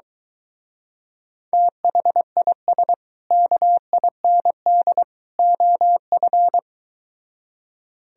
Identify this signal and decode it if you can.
Morse code, heard as a keyed tone, about 22 wpm, 700 Hz. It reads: THIS KIND OF